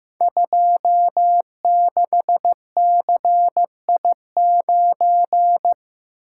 Morse code, heard as a keyed tone, about 15 wpm, 700 Hz. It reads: 26CI9
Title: Code Groups - 5 Letter:Numbers - 15wpm